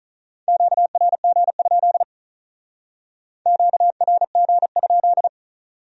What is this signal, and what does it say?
Morse code, heard as a keyed tone, about 46 wpm, 700 Hz. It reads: QRG? QRG?